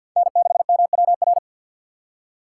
Morse code, heard as a keyed tone, about 53 wpm, 700 Hz. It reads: N6MWR
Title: USA Call Signs - 5 Characters - 4x Character Spacing - 50wpm